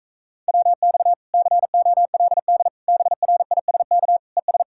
Morse code, heard as a keyed tone, about 38 wpm, 700 Hz. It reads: WX COLD BRISK ES